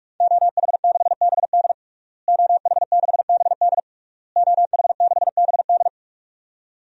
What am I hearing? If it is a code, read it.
Morse code, heard as a keyed tone, about 46 wpm, 700 Hz. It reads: OH6BD OH6BD OH6BD
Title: DXCC-100 Call Signs - 5 Characters - Repeated 3x - 45wpm